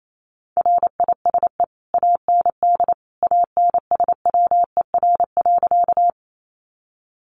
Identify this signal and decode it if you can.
Morse code, heard as a keyed tone, about 28 wpm, 700 Hz. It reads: RISE AND ANSWER.
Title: Sentences from Top 900 Words 28wpm